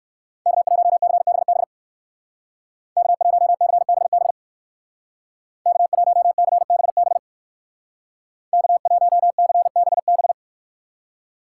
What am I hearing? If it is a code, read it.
Morse code, heard as a keyed tone, about 55 wpm, 700 Hz. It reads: K1CBB K1CBB K1CBB K1CBB
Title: Speed Racing - USA 1x3 Call Signs 45wpm